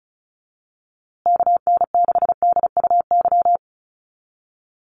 Morse code, heard as a keyed tone, about 34 wpm, 700 Hz. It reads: KN6DUY
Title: USA Call Signs Repeated 3x 35wpm